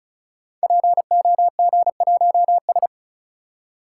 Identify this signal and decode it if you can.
Morse code, heard as a keyed tone, about 36 wpm, 700 Hz. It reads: POG1S
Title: Code Groups - 5 Letter:Numbers - 35wpm